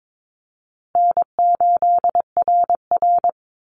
Morse code, heard as a keyed tone, about 22 wpm, 700 Hz. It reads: N8RR
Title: Easy Call Signs Repeated 3x - 22wpm